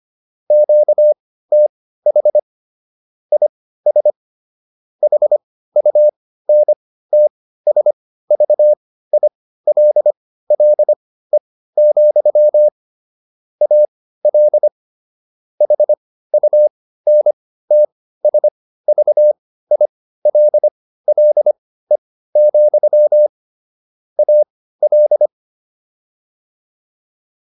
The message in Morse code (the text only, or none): QTH IS HUNTSVILLE, AL HUNTSVILLE, AL